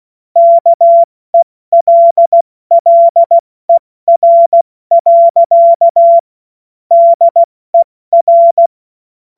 Morse code, heard as a keyed tone, about 16 wpm, 680 Hz. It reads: KELLER. DER
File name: Witz-16-September-2025-70-F80